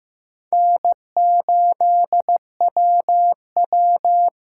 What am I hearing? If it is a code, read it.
Morse code, heard as a keyed tone, about 16 wpm, 700 Hz. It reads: N8WW
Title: USA Callsign DE Callsign - 4 Characters - 15wpm